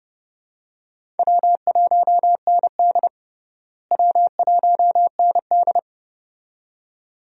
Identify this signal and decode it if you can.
Morse code, heard as a keyed tone, about 31 wpm, 700 Hz. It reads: W1ND W1ND
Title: USA Call Signs - 4 Characters - Repeated 2x - 30wpm